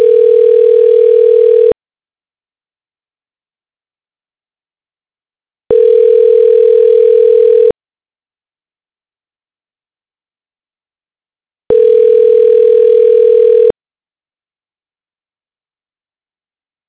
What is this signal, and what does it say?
Ringback tone (North America), 440 + 480 Hz.